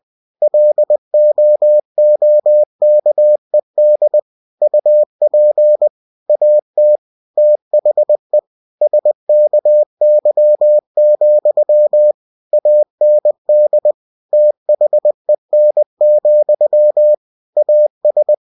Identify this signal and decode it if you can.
Morse code, heard as a keyed tone, about 20 wpm, 600 Hz. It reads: LOOKED UP AT THE SKY, AND THEN, AS
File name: cw-books / winnie-the-pooh / winnie-20wpm-page-0024